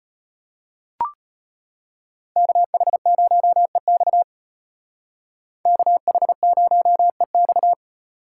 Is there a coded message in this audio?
Yes: Morse code from 2.4 to 7.7 s, heard as a keyed tone, about 32 wpm, 700 Hz. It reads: KH0EX KH0EX